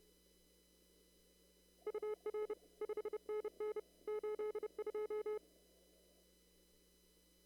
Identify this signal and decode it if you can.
Morse code, heard as a keyed tone, about 31 wpm, 420 Hz. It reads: UR 5NN 82
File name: CW / kt5j wpx cw-op-k5pi